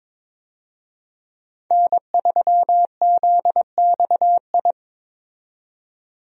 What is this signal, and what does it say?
Morse code, heard as a keyed tone, about 22 wpm, 700 Hz. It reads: N3ZXI
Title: USA Call Signs - 5 Characters - 22wpm